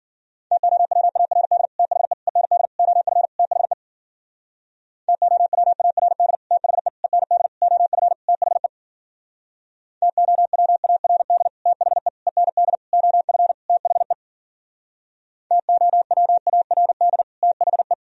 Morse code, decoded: TOWARD THE END OF THE TOWARD THE END OF THE TOWARD THE END OF THE TOWARD THE

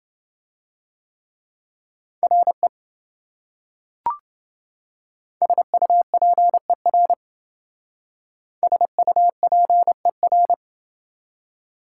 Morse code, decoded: RE SUPER SUPER

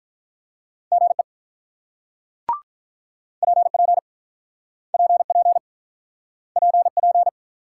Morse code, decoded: GE PP PP PP

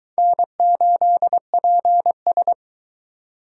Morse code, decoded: N8PS

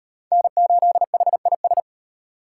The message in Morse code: N8HIS

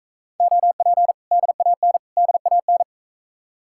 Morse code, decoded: OP DAN DAN

